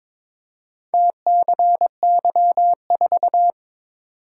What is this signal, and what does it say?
Morse code, heard as a keyed tone, about 22 wpm, 700 Hz. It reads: TCY4
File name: practice / 22wpm / Four Letter-Number Combinations - 22wpm